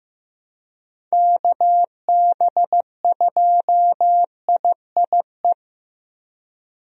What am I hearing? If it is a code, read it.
Morse code, heard as a keyed tone, about 15 wpm, 700 Hz. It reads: KB2IIE
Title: USA Call Signs - 6 Characters - 15wpm